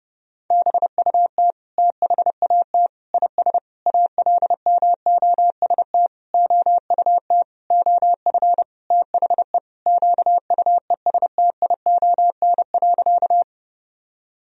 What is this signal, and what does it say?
Morse code, heard as a keyed tone, about 30 wpm, 700 Hz. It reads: BUT THAT IS ALMOST OUT OF THE QUESTION.
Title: Sentences from Top 1000 Words - Review Part 4 - 30wpm